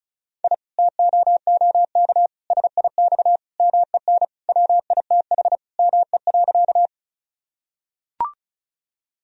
Morse code, heard as a keyed tone, about 36 wpm, 700 Hz. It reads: I TOOK SIX MEN WITH ME.
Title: Sentences from Top 1000 Words - Review Part 2 - 35wpm